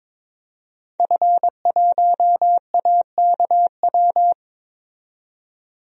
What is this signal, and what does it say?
Morse code, heard as a keyed tone, about 22 wpm, 700 Hz. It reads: F1AKW
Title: DXCC-100 Call Signs - 5 Characters - 22wpm